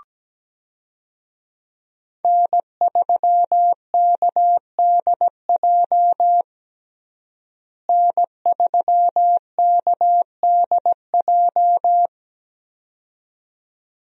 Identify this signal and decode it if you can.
Morse code, heard as a keyed tone, about 17 wpm, 700 Hz. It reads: N3KDJ N3KDJ
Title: USA Call Signs - 5 Characters - Repeated 2x - 17wpm